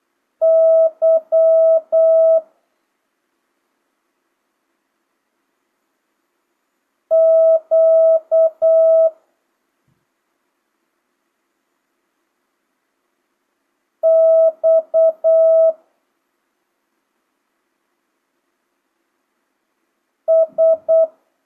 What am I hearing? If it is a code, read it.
Morse code, heard as a keyed tone, about 8 wpm, 650 Hz. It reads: Y Q X S
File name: jh8cbh / 2025cw